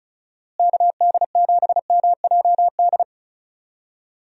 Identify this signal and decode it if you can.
Morse code, heard as a keyed tone, about 36 wpm, 700 Hz. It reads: KD7MJD
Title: USA Call Signs - 6 Characters - Repeated 3x - 35wpm